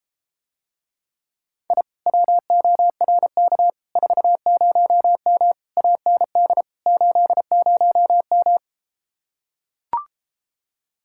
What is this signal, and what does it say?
Morse code, heard as a keyed tone, about 33 wpm, 700 Hz. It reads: I WORK 40M AND 80M